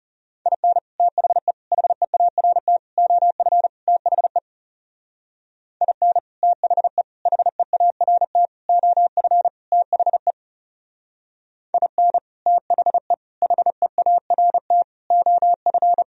Morse code, heard as a keyed tone, about 34 wpm, 700 Hz. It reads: IN THE HEART OF THE IN THE HEART OF THE IN THE HEART OF
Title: Speed-Racing - Sets of 5 Words 30wpm